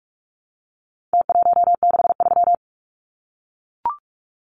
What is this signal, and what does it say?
Morse code, heard as a keyed tone, about 44 wpm, 700 Hz. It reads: T163
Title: Numbers - Contest Style - 4-Digit 45wpm